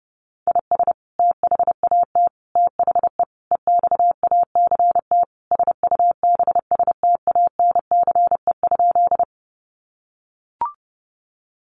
Morse code, heard as a keyed tone, about 30 wpm, 700 Hz. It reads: IS THAT THE EXACT SUBSTANCE?
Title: Sentences from Top 1500 Words 30wpm